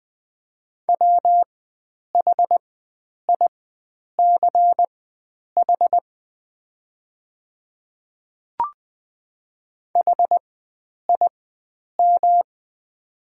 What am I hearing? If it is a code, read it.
Morse code, heard as a keyed tone, about 20 wpm, spaced as for 9 wpm, 700 Hz. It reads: WHICH HIM
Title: 4x Character Spacing - Top 100 Words - 20wpm